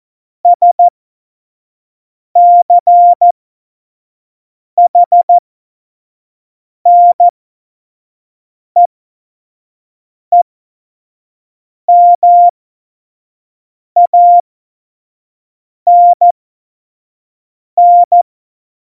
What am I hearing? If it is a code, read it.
Morse code, heard as a keyed tone, about 14 wpm, spaced as for 5 wpm, 700 Hz. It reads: SCHNEEMANN
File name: Morsekurs / Witze / Witz 07 April 2020, 25 F70